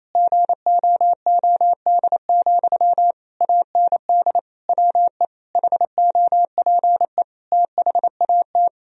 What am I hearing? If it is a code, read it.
Morse code, heard as a keyed tone, about 27 wpm, 700 Hz. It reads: GOOD, AND WE HOPE THAT
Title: Sentences from Top 1000 Words - Review Part 1 - 28wpm